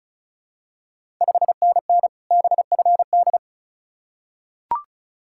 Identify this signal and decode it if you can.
Morse code, heard as a keyed tone, about 36 wpm, 700 Hz. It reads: 5NN BFD
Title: Ontario QSO Party Exchange 35wpm